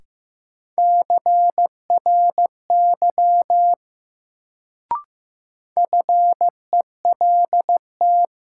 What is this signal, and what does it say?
Morse code, heard as a keyed tone, about 15 wpm, 700 Hz. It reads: CRY FELT
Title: Sets of 5 Words Encore - New Word Drill - 15wpm